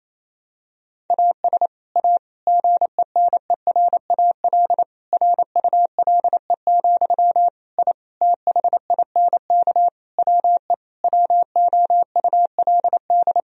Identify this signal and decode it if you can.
Morse code, heard as a keyed tone, about 28 wpm, 700 Hz. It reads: AS A GENERAL RULE, I THINK WE WOULD
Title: Sentences from Top 1000 Words - Review Part 3 - 28wpm